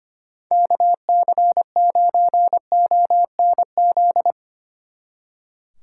Morse code, heard as a keyed tone, about 32 wpm, 700 Hz. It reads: KC9ONZ